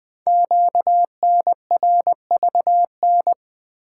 Morse code, heard as a keyed tone, about 20 wpm, 700 Hz. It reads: QNRVN
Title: Code Groups - 5 Letters - 20wpm